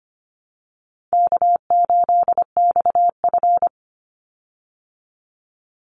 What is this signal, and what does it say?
Morse code, heard as a keyed tone, about 25 wpm, 700 Hz. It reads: K8XF
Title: Call Signs Repeated 3x - 25wpm